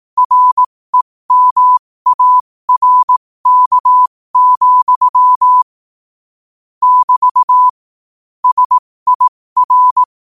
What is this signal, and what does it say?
Morse code, heard as a keyed tone, about 18 wpm, spaced as for 15 wpm, 1000 Hz. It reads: REMARK, <BT> SIR
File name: qotdfiles / 15wpm